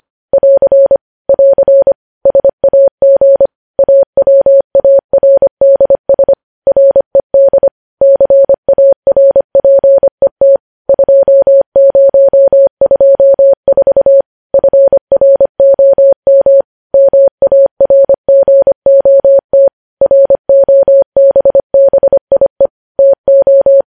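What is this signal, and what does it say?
Morse code, heard as a keyed tone, about 25 wpm, 550 Hz. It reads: <AR> <AR> SAG AWARDS RED CARPET 2024 FROM MARGOT ROBBIE TO
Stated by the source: Morse code news headlines sent at 25 wpm.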